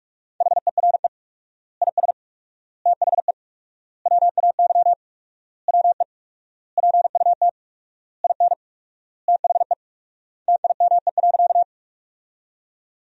Morse code, heard as a keyed tone, about 44 wpm, 700 Hz. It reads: HERE IS THE WAY WE PUT IN THE TIME.
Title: 4x Word-Spacing - Sentences from Top 200 Words - 45wpm